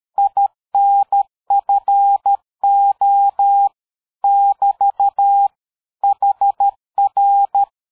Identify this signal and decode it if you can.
Morse code, heard as a keyed tone, about 13 wpm, 800 Hz. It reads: INFO <BT> HR
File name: dk5ke / QSO 60BpM